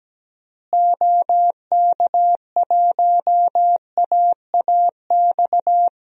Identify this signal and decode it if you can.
Morse code, heard as a keyed tone, about 17 wpm, 700 Hz. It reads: OK1AAX
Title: DXCC-100 Call Signs - 6 Characters - Repeated 2x - 17wpm